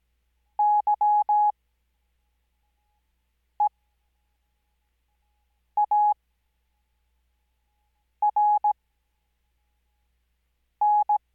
Morse code, heard as a keyed tone, about 17 wpm, spaced as for 4 wpm, 840 Hz. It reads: YEARN
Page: Farnsworth Spacing: 15WPM Character speed Spaced at 5WPM.